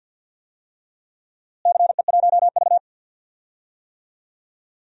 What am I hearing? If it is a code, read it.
Morse code, heard as a keyed tone, about 51 wpm, 700 Hz. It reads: CE1V
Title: DXCC-100 Call Signs - 4 Characters - Repeated 2x - 50wpm